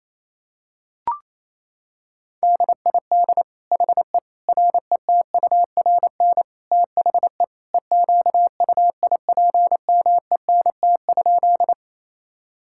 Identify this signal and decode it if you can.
Morse code, heard as a keyed tone, about 28 wpm, 700 Hz. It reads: DID HE RETURN THE EQUIPMENT?
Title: Sentences from Top 1000 Words - Review Part 5 - 28wpm